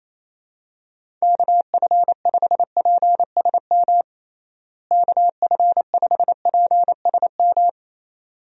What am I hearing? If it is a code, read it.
Morse code, heard as a keyed tone, about 28 wpm, 700 Hz. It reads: KF5PSM KF5PSM
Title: USA Call Signs - 6 Characters - Repeated 2x - 28wpm